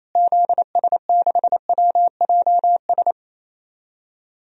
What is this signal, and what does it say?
Morse code, heard as a keyed tone, about 28 wpm, 700 Hz. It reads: ZS6WJS
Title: DXCC-100 Call Signs - 6 Characters - Repeated 3x - 28wpm